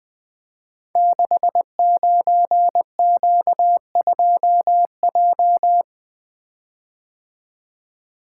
Morse code, heard as a keyed tone, about 21 wpm, 700 Hz. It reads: 69Q2J
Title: Code Groups - 5 Letter:Numbers - 20wpm